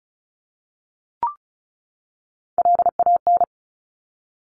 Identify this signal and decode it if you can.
Morse code, heard as a keyed tone, about 36 wpm, 700 Hz. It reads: LAN
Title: Top Three Letter Combinations in Words 35wpm